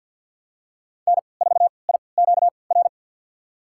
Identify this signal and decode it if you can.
Morse code, heard as a keyed tone, about 52 wpm, spaced as for 29 wpm, 700 Hz. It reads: N4IQR
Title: USA Call Signs - 5 Characters - 3x Character Spacing - 50wpm